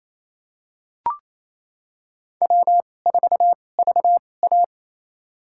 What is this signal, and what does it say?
Morse code, heard as a keyed tone, about 27 wpm, spaced as for 20 wpm, 700 Hz. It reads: W4VA
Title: USA Call Signs - 4 Characters - 2x Character Spacing - 28wpm